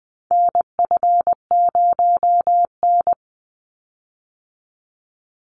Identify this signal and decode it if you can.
Morse code, heard as a keyed tone, about 20 wpm, 700 Hz. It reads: NF0N
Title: Call Signs Repeated 3x - 20wpm